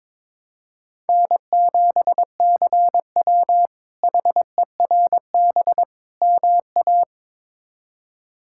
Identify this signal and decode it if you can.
Morse code, heard as a keyed tone, about 21 wpm, 700 Hz. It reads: N7CW HERB MA
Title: K1USN SST QSO Practice - 22wpm